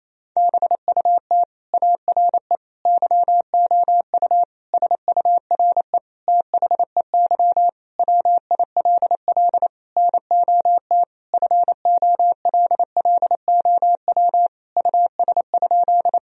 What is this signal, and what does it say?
Morse code, heard as a keyed tone, about 28 wpm, 700 Hz. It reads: BUT ARE YOU SURE THEY WILL NOT FOLLOW US?
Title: Sentences from Top 1000 Words - Review Part 1 - 28wpm